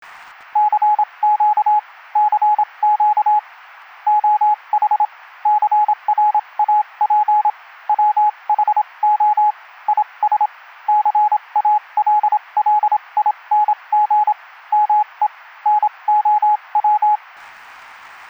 Morse code, heard as a keyed tone, about 27 wpm, spaced as for 23 wpm, 840 Hz, with radio noise. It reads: CQ CQ OH CRAP WHO IS CALLING ME NOW
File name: Sound FX Collection / morse-clean